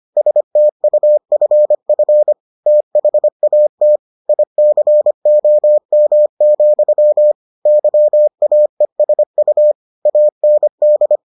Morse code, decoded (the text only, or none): STUFF THAT ICOM, YAESU AND